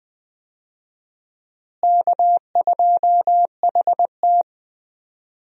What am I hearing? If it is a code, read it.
Morse code, heard as a keyed tone, about 20 wpm, 700 Hz. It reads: K2HT
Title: USA Call Signs - 4 Characters - Repeated 3x - 20wpm